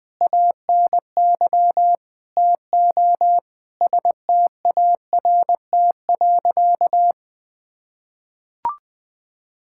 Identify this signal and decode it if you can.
Morse code, heard as a keyed tone, about 20 wpm, 700 Hz. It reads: ANY TO START.
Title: Sentences with 8-Word Phrases - Spoken Beginning - 20wpm